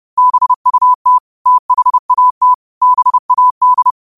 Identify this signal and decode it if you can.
Morse code, heard as a keyed tone, about 30 wpm, 1000 Hz. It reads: BUT THAT BAD